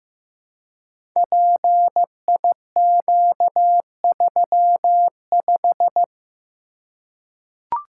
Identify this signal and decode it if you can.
Morse code, heard as a keyed tone, about 15 wpm, 700 Hz. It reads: PIQ35
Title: Code Groups - 5 Letter:Numbers - 15wpm